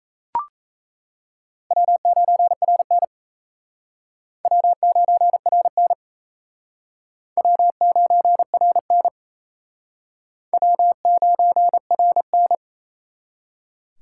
Morse code, heard as a keyed tone, about 36 wpm, 700 Hz. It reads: W9RN W9RN W9RN W9RN